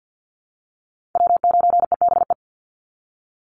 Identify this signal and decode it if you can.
Morse code, heard as a keyed tone, about 49 wpm, 700 Hz. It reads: W9EBE
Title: Call Signs Repeated 3x - 50wpm